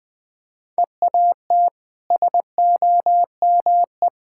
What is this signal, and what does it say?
Morse code, heard as a keyed tone, about 21 wpm, 700 Hz. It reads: EAT SOME
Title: Easy Sentences with Hints - 20wpm